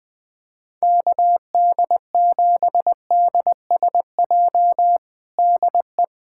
Morse code, heard as a keyed tone, about 20 wpm, 700 Hz. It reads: KD7DSJ DE
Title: USA Callsign DE Callsign 20wpm